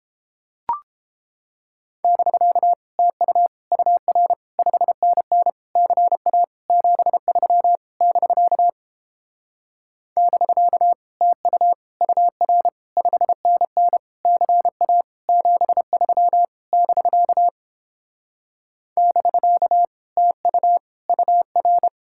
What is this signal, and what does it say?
Morse code, heard as a keyed tone, about 29 wpm, 700 Hz. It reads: <BK> TU UR 5NN CA 73 <BK> <BK> TU UR 5NN CA 73 <BK> <BK> TU UR